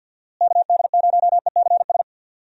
Morse code, heard as a keyed tone, about 52 wpm, 700 Hz. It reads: KD0ECS
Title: USA Call Signs - 6 Characters - Repeated 3x - 50wpm